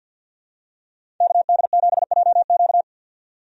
Morse code, heard as a keyed tone, about 51 wpm, 700 Hz. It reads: KD7JQ
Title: USA Call Signs - 5 Characters - 50wpm